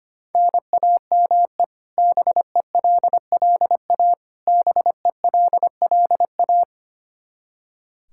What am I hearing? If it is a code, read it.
Morse code, heard as a keyed tone, about 25 wpm, 700 Hz. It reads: NAME BELLA BELLA
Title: Practice QSO - Exchange 1 - Beginning - 25wpm